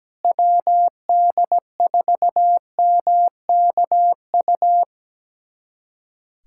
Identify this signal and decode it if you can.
Morse code, heard as a keyed tone, about 17 wpm, 700 Hz. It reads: WD4MKU
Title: USA Callsign DE Callsign - 6 Characters - 17wpm